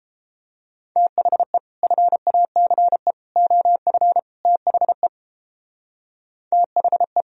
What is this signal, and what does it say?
Morse code, heard as a keyed tone, about 29 wpm, 700 Hz. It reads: THE FACE OF THE THE